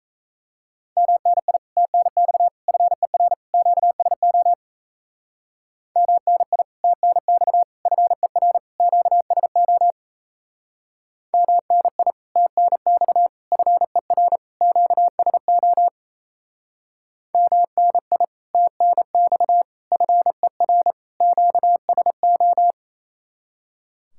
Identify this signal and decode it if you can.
Morse code, heard as a keyed tone, about 36 wpm, 700 Hz. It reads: MNI TNX FER QSO MNI TNX FER QSO MNI TNX FER QSO MNI TNX FER QSO